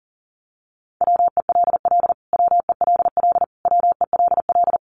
Morse code, heard as a keyed tone, about 40 wpm, 700 Hz. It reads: WELL WELL WELL
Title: Top 200 Words Repeated 40wpm